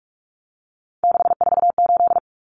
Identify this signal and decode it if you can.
Morse code, heard as a keyed tone, about 44 wpm, 700 Hz. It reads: 648
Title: Numbers - 3-Digit 45wpm